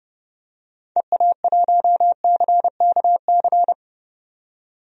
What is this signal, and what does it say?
Morse code, heard as a keyed tone, about 30 wpm, 700 Hz. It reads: EA1CKC
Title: DXCC-100 Call Signs - 6 Characters - 30wpm